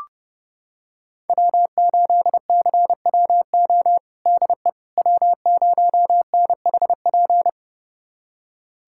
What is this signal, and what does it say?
Morse code, heard as a keyed tone, about 30 wpm, 700 Hz. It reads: W8CWO DE W0NHP
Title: USA Callsign DE Callsign - 5 Characters - 30wpm